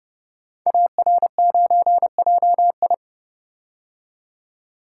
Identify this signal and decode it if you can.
Morse code, heard as a keyed tone, about 30 wpm, 700 Hz. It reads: AR9JI
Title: Code Groups - 5 Letter:Numbers - 30wpm